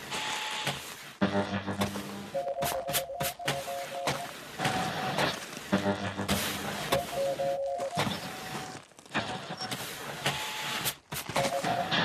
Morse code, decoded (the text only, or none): none